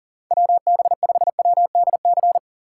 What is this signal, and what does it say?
Morse code, heard as a keyed tone, about 40 wpm, 700 Hz. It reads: WB5WDC
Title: USA Call Signs - 6 Characters - 3x Character Spacing - 40wpm